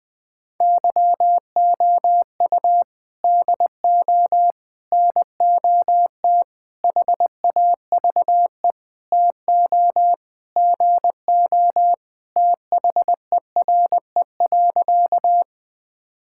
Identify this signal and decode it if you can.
Morse code, heard as a keyed tone, about 19 wpm, 700 Hz. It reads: YOU DO NOT HAVE TO GO THERE.
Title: Sentences with 5-Word Phrases - Spoken Beginning - 20wpm